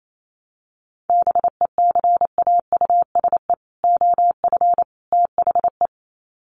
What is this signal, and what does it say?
Morse code, heard as a keyed tone, about 28 wpm, 700 Hz. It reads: BECAUSE OF THE
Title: Sets of 3 Words 28wpm